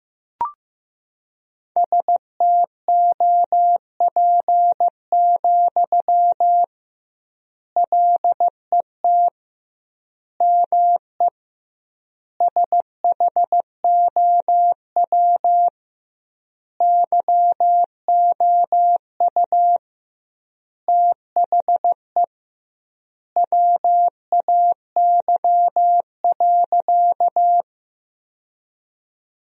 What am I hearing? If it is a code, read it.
Morse code, heard as a keyed tone, about 16 wpm, 700 Hz. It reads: STOP, LET ME SHOW YOU THE WAY.
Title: 2x Word-Spacing - Sentences from Top 300 Words - 15wpm